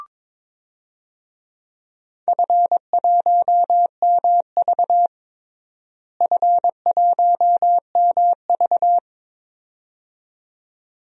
Morse code, heard as a keyed tone, about 22 wpm, 700 Hz. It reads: F1MV F1MV
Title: DXCC-100 Call Signs - 4 Characters - Repeated 2x - 22wpm